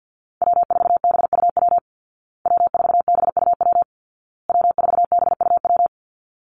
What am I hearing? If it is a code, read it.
Morse code, heard as a keyed tone, about 50 wpm, 700 Hz. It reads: W4BUW W4BUW W4BUW
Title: Call Signs Repeated 2x - 50wpm